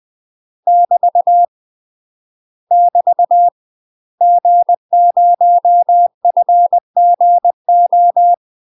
Morse code, heard as a keyed tone, about 20 wpm, 700 Hz. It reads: <BT> <BT> G0FGO
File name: UK MASTER QSOs TRAINING / 20WPM / UK MASTER QSOs 0022